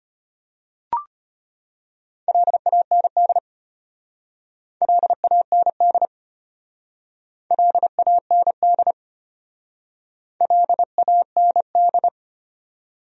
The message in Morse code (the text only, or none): LAND LAND LAND LAND